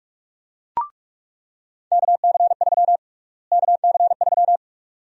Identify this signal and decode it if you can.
Morse code, heard as a keyed tone, about 45 wpm, 700 Hz. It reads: KC3 KC3
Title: USA Call Sign Prefixes - Repeated 2x - 45wpm